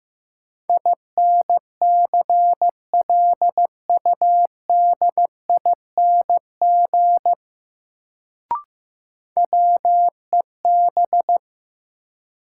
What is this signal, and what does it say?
Morse code, heard as a keyed tone, about 15 wpm, 700 Hz. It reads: INCLUDING WEB
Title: Sentences from Top 1000 Words - Review Part 4 - New Word Drill - 15wpm